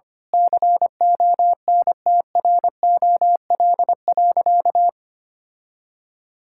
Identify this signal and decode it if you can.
Morse code, heard as a keyed tone, about 25 wpm, 700 Hz. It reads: CONTROL.
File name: practice / 25wpm / Sentences with 9-Word Phrases - 25wpm